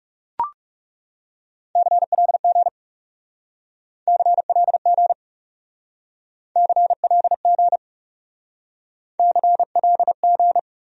Morse code, heard as a keyed tone, about 37 wpm, 700 Hz. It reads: CLG CLG CLG CLG
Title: Speed-Racing Abbreviations - 30wpm